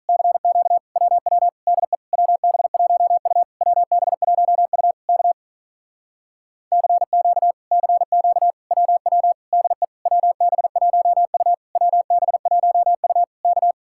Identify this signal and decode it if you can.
Morse code, heard as a keyed tone, about 47 wpm, 700 Hz. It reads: CQ WW DE WB1U WB1U K CQ CQ WW DE WB1U WB1U K